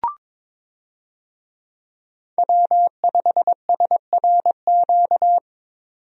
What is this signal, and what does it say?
Morse code, heard as a keyed tone, about 22 wpm, 700 Hz. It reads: W5SRQ
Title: USA Call Signs - 5 Characters - 22wpm